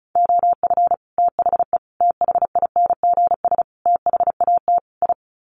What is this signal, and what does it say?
Morse code, heard as a keyed tone, about 34 wpm, 700 Hz. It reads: OF THE THINGS THAT I
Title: Sets of 5 Words Encore 35wpm